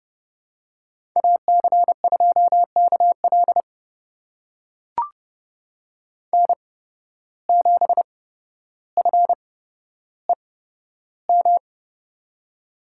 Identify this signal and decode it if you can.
Morse code, heard as a keyed tone, about 30 wpm, spaced as for 13 wpm, 700 Hz. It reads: AC2KL N7FEM
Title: USA Call Signs - 5 Characters - 8x Character Spacing - 30wpm